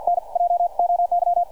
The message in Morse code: NJOY